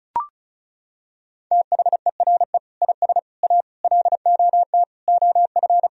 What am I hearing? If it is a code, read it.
Morse code, heard as a keyed tone, about 34 wpm, 700 Hz. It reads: THERE IS A LOT OF
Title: Sentences from Top 1000 Words - Review Part 4 - 35wpm